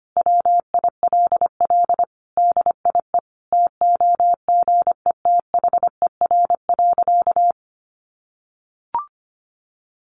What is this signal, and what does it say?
Morse code, heard as a keyed tone, about 25 wpm, 700 Hz. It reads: WILL DIE TOGETHER.
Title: Sentences from Top 1300 Words 25wpm